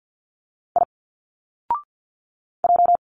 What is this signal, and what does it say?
Morse code, heard as a keyed tone, about 46 wpm, 700 Hz. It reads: I J
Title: Single Letters 50wpm